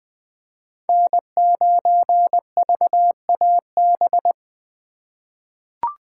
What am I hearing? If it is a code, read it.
Morse code, heard as a keyed tone, about 20 wpm, 700 Hz. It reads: N9VAB
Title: USA Call Signs - 5 Characters - 2x Character Spacing - 20wpm